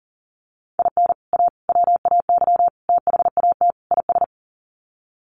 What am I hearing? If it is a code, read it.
Morse code, heard as a keyed tone, about 40 wpm, 700 Hz. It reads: IN A WAY THAT IS
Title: Sets of 5 Words Encore 40wpm